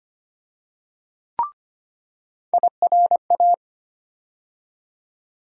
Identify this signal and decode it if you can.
Morse code, heard as a keyed tone, about 25 wpm, 700 Hz. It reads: IRA
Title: USA Call Sign Postfixes - 3 Letters 25wpm